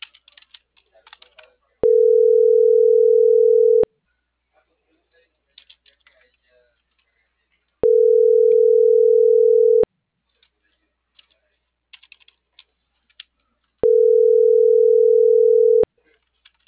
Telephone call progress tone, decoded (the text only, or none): ringback tone